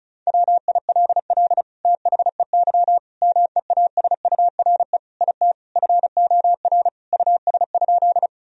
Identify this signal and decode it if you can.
Morse code, heard as a keyed tone, about 34 wpm, 700 Hz. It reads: WILL THEY MEASURE IT FOR US?
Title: Sentences from Top 1000 Words - Review Part 2 - 35wpm